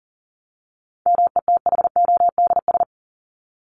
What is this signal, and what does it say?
Morse code, heard as a keyed tone, about 40 wpm, 700 Hz. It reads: METHODS
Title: ES:S Suffix - Top 100 Words with Base Words 40wpm